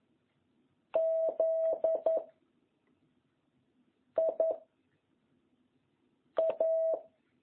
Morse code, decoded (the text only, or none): Z I A